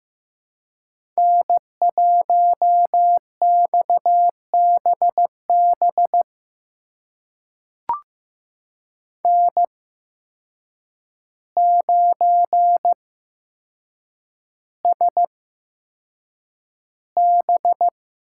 Morse code, heard as a keyed tone, about 15 wpm, spaced as for 7 wpm, 700 Hz. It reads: N1XBB N9SB
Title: USA Call Signs - 5 Characters - 8x Character Spacing - 15wpm